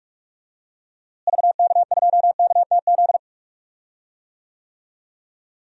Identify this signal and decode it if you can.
Morse code, heard as a keyed tone, about 46 wpm, 700 Hz. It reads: VK2KTZ
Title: DXCC-100 Call Signs - 6 Characters - Repeated 2x - 45wpm